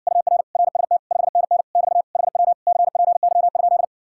Morse code, heard as a keyed tone, about 60 wpm, 700 Hz. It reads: UR RST 5NN <BT> HW CPY?